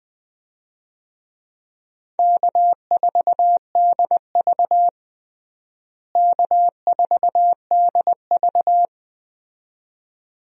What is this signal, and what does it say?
Morse code, heard as a keyed tone, about 20 wpm, 700 Hz. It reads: K4DV K4DV
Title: USA Call Signs - 4 Characters - Repeated 2x - 20wpm